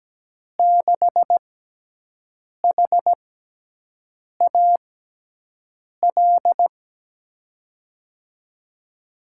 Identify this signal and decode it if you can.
Morse code, heard as a keyed tone, about 17 wpm, spaced as for 6 wpm, 700 Hz. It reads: 6HAL
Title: USA Call Signs - 6 Characters - 6x Character Spacing - 17wpm